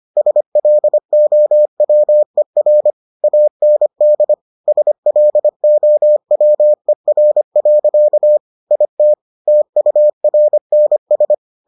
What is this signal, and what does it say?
Morse code, heard as a keyed tone, about 25 wpm, 600 Hz. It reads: SLOWER AND SLOWER. IT TURNS